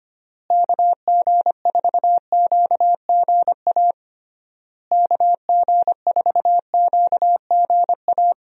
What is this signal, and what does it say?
Morse code, heard as a keyed tone, about 25 wpm, 700 Hz. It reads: KG4QGA KG4QGA
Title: USA Call Signs - 6 Characters - Repeated 3x - 25wpm